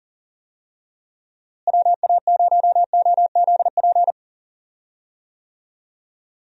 Morse code, heard as a keyed tone, about 39 wpm, 700 Hz. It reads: WA0OZP
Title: USA Call Signs - 6 Characters - Repeated 2x - 40wpm